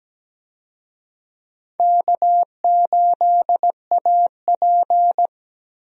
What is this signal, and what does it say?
Morse code, heard as a keyed tone, about 17 wpm, 700 Hz. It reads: K8AP
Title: USA Call Signs - 4 Characters - Repeated 2x - 17wpm